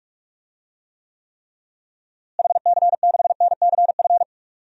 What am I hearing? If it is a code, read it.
Morse code, heard as a keyed tone, about 44 wpm, 700 Hz. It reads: HC6NCF
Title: DXCC-100 Call Signs - 6 Characters - Repeated 3x - 45wpm